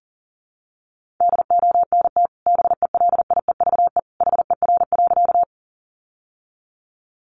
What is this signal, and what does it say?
Morse code, heard as a keyed tone, about 40 wpm, 700 Hz. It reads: DONT BELIEVE HER.
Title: Sentences from Top 1000 Words 40wpm